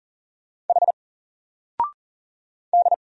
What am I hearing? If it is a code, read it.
Morse code, heard as a keyed tone, about 40 wpm, 700 Hz. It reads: H D
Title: Single Letter-Number 40wpm